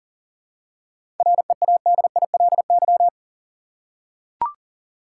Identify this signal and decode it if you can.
Morse code, heard as a keyed tone, about 40 wpm, 700 Hz. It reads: READILY
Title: Seven Letter Words - 40wpm